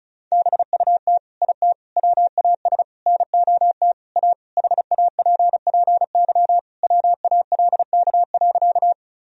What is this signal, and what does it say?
Morse code, heard as a keyed tone, about 36 wpm, 700 Hz. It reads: BUT IT WAS NOT A HAPPY WALK.
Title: Sentences from Top 1000 Words - Review Part 1 - 35wpm